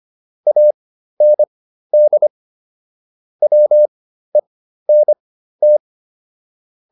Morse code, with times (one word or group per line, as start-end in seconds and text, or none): 0.5-2.3 AND
3.4-5.8 WENT